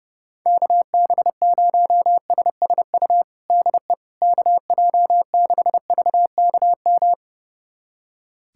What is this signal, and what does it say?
Morse code, heard as a keyed tone, about 30 wpm, 700 Hz. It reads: KB0SSU DE KJ6VKM
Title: USA Callsign DE Callsign - 6 Characters - 30wpm